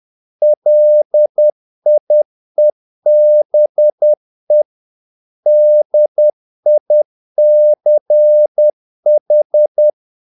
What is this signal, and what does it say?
Morse code, heard as a keyed tone, about 10 wpm, 600 Hz. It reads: LIEBE DICH
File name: dk5ke / Humor-50BpM